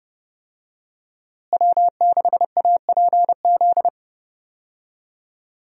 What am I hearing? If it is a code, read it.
Morse code, heard as a keyed tone, about 32 wpm, 700 Hz. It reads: W6APZ
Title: USA Call Signs - 5 Characters - Repeated 3x - 30wpm